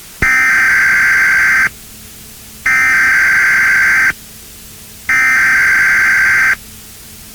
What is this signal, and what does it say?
Emergency Alert System (SAME) tones (header bursts), heard as harsh buzzing bursts. It is heard from 0.2 to 6.5 s.